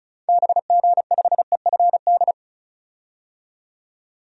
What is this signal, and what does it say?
Morse code, heard as a keyed tone, about 36 wpm, 700 Hz. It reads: BG5EFD
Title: DXCC-100 Call Signs - 6 Characters - 35wpm